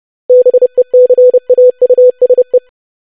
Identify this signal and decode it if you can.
Morse code, heard as a keyed tone, about 30 wpm, 500 Hz. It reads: BECAUSE
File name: basic500-25wpm